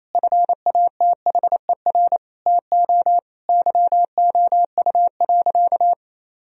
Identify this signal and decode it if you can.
Morse code, heard as a keyed tone, about 28 wpm, 700 Hz. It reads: FATHER TO YOU.
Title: Sentences from Top 1000 Words - Review Part 1 - 28wpm